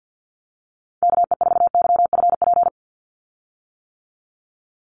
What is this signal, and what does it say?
Morse code, heard as a keyed tone, about 50 wpm, 700 Hz. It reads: KE4YFP
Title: USA Call Signs 50wpm